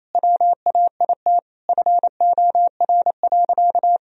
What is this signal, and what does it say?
Morse code, heard as a keyed tone, about 28 wpm, 700 Hz. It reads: WAIT FOR.
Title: Sentences from Top 1000 Words - Review Part 3 - 28wpm